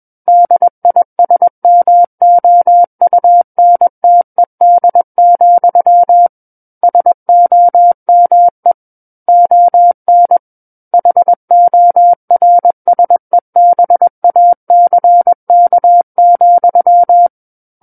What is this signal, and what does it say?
Morse code, heard as a keyed tone, about 21 wpm, 700 Hz. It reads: DISMOUNTED, SOME ON HORSEBACK,
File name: Book One / 12-Chapter 12 - 21 WPM